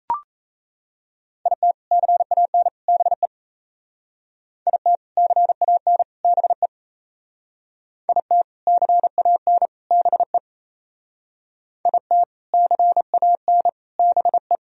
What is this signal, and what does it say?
Morse code, heard as a keyed tone, about 34 wpm, 700 Hz. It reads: IT CAN BE IT CAN BE IT CAN BE IT CAN BE
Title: Speed-Racing - Sets of 3 Words 28wpm